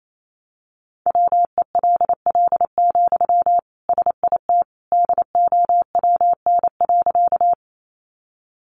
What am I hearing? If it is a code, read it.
Morse code, heard as a keyed tone, about 28 wpm, 700 Hz. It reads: WELL, SIT DOWN.
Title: Sentences from Top 500 Words 28wpm